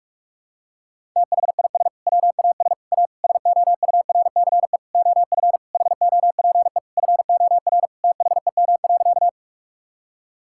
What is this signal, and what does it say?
Morse code, heard as a keyed tone, about 46 wpm, 700 Hz. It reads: THIS WAS A SOURCE OF HOPE FOR THEM.
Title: Sentences from Top 1000 Words - Review Part 5 - 45wpm